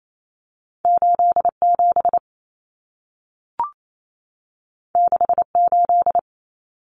Morse code, heard as a keyed tone, about 28 wpm, 700 Hz. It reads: 87 68
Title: Numbers - 2-Digit 28wpm